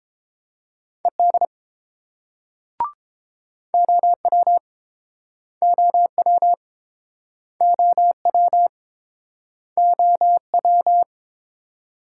Morse code, heard as a keyed tone, about 28 wpm, 700 Hz. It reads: ED OW OW OW OW